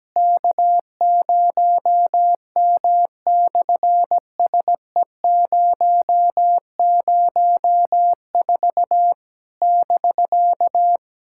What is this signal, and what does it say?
Morse code, heard as a keyed tone, about 17 wpm, 700 Hz. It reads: K0M/SE004 <BK>
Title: SOTA QSO Practice 17wpm